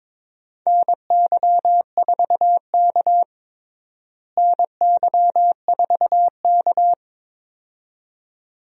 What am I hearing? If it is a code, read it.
Morse code, heard as a keyed tone, about 22 wpm, 700 Hz. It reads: NY4K NY4K
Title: USA Call Signs - 4 Characters - Repeated 2x - 22wpm